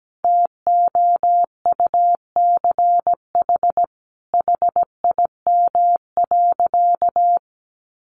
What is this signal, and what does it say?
Morse code, heard as a keyed tone, about 17 wpm, 700 Hz. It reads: TOUCH HIM.
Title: Sentences from Top 1300 Words 17wpm